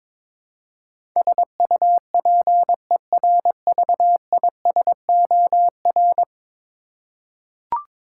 Morse code, heard as a keyed tone, about 22 wpm, 700 Hz. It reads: SUPERVISOR
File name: practice / 22wpm / 10 Letter Words - First 3 Letters Given - 22wpm